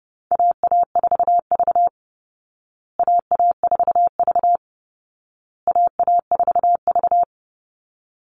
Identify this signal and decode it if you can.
Morse code, heard as a keyed tone, about 30 wpm, 700 Hz. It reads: AA4V AA4V AA4V
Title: Call Signs Repeated 2x - 30wpm